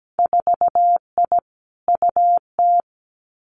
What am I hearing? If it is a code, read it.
Morse code, heard as a keyed tone, about 17 wpm, 700 Hz. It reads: 4I UT
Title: Winter Field Day Exchanges Encore 17wpm